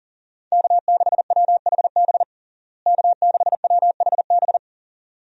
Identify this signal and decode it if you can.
Morse code, heard as a keyed tone, about 40 wpm, 700 Hz. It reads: K6WHB K6WHB
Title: USA Call Signs - 5 Characters - Repeated 2x - 40wpm